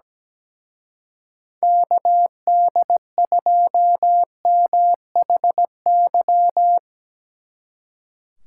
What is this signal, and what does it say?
Morse code, heard as a keyed tone, about 18 wpm, 700 Hz. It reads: KD2MHY
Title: Speed Racing - USA 2x3 Call Signs 17wpm